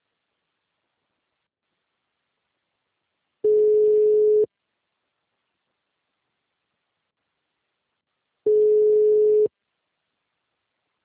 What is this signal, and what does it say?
Ringback tone (Europe, 425 Hz).